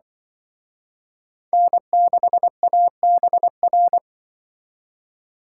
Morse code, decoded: N6ABR

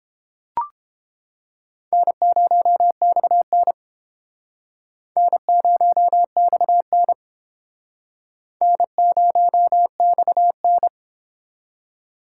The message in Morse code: N0XN N0XN N0XN

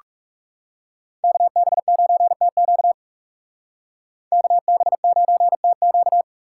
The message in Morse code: KB9TQ KB9TQ

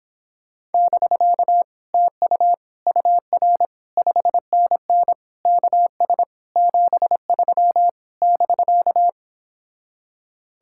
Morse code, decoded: <BK> TU UR 5NN KS 73 <BK>

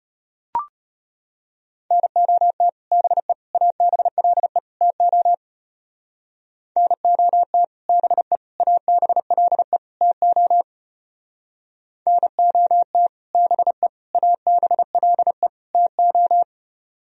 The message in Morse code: NOT BE ABLE TO NOT BE ABLE TO NOT BE ABLE TO

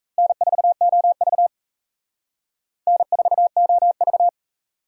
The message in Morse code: N4OV N4OV

